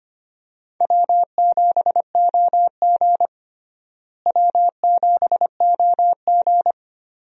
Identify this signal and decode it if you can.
Morse code, heard as a keyed tone, about 25 wpm, 700 Hz. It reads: W7OG W7OG
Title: USA Call Signs - 4 Characters - Repeated 3x - 25wpm